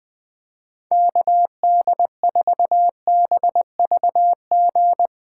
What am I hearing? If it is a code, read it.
Morse code, heard as a keyed tone, about 20 wpm, 700 Hz. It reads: KD4BVG
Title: 2023 POTA Activator Call Signs - 20wpm